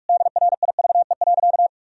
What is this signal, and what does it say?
Morse code, heard as a keyed tone, about 42 wpm, 700 Hz. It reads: DRIVE.
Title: Sentences from Top 1000 Words - Review Part 2 - 45wpm